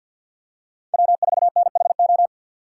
Morse code, heard as a keyed tone, about 53 wpm, 700 Hz. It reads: W4NHO
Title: USA Call Signs - 5 Characters - 6x Character Spacing - 50wpm